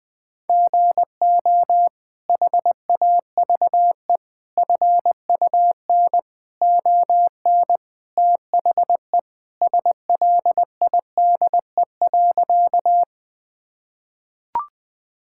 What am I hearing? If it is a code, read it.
Morse code, heard as a keyed tone, about 20 wpm, 700 Hz. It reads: GO HAVE FUN ON THE SLIDE.
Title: Sentences from Top 1600 Words 20wpm